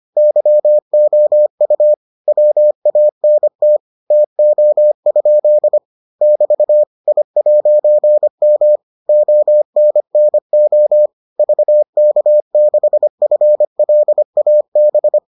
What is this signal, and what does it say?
Morse code, heard as a keyed tone, about 25 wpm, 600 Hz. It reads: YOU WANT TO? <BT> I'M ONNO VK6FLAB